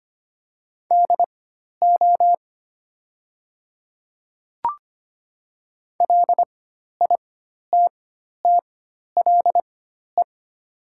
Morse code, heard as a keyed tone, about 25 wpm, spaced as for 12 wpm, 700 Hz. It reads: DO LITTLE
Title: 4x Character Spacing - Top 300 Words - 25wpm